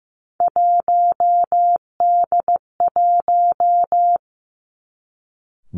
Morse code, heard as a keyed tone, about 15 wpm, 700 Hz. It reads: 1D1
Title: Three Challenging Letter-Number Combinations 15wpm